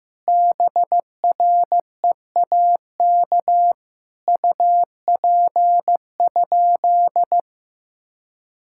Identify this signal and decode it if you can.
Morse code, heard as a keyed tone, about 15 wpm, 700 Hz. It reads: BREAK UP?
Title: Sentences from Top 1000 Words - Review Part 6 - 15wpm